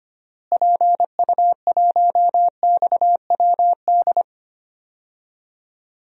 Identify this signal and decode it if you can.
Morse code, heard as a keyed tone, about 25 wpm, 700 Hz. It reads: PU1XWD
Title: DXCC-100 Call Signs - 6 Characters - Repeated 2x - 25wpm